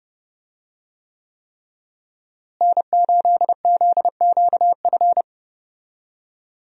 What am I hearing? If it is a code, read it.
Morse code, heard as a keyed tone, about 30 wpm, 700 Hz. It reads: N8ZQF
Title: USA Call Signs - 5 Characters - Repeated 3x - 30wpm